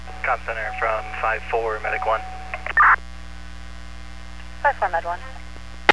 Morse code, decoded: none

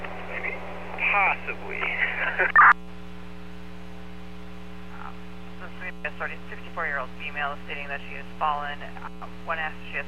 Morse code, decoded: none

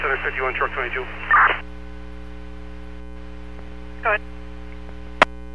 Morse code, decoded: none

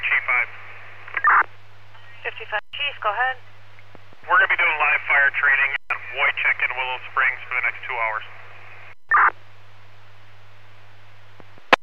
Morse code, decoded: none

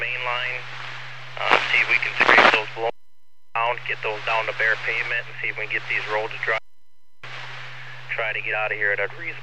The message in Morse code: none

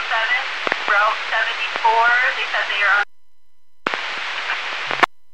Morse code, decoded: none

